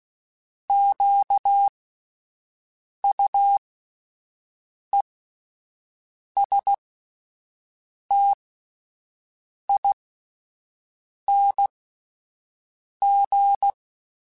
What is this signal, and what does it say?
Morse code, heard as a keyed tone, about 16 wpm, spaced as for 5 wpm, 800 Hz. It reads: QUESTING
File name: Alphabetical / Alphabetical mp3s / Qs @5